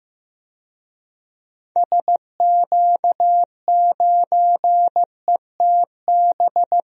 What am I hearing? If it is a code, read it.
Morse code, heard as a keyed tone, about 15 wpm, 700 Hz. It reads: SQ9ETB
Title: DXCC-100 Call Signs - 6 Characters - Repeated 2x - 15wpm